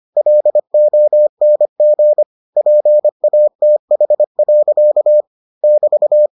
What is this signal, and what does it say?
Morse code, heard as a keyed tone, about 25 wpm, 600 Hz. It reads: LONG PATH. <BT>